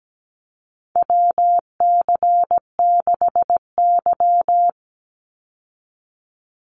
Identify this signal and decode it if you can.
Morse code, heard as a keyed tone, about 17 wpm, 700 Hz. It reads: WC6Y
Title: Call Signs Repeated 2x - 17wpm